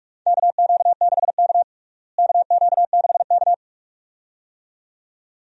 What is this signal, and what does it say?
Morse code, heard as a keyed tone, about 46 wpm, 700 Hz. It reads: KQ6K KQ6K
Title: USA Call Signs - 4 Characters - Repeated 2x - 45wpm